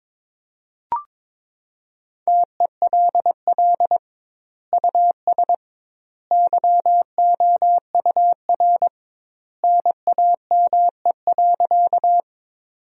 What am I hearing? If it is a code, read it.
Morse code, heard as a keyed tone, about 22 wpm, 700 Hz. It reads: TELL US YOUR NAME.
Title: 2x Word-Spacing - Sentences from Top 200 Words - 22wpm